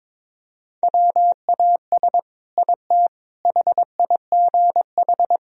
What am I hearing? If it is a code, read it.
Morse code, heard as a keyed tone, about 22 wpm, 700 Hz. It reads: WAS IT HIGH
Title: Sentences from Top 1000 Words - Review Part 4 - 22wpm